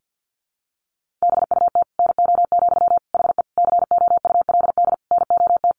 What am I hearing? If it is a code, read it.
Morse code, heard as a keyed tone, about 49 wpm, 700 Hz. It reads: BUT NO, HE COULD NOT
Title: Sentences from Top 1400 Words 50wpm